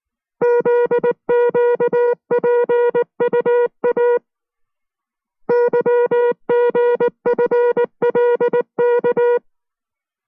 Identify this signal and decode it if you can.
Morse code, heard as a keyed tone, about 19 wpm, 480 Hz. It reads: ZQPUA YGFLK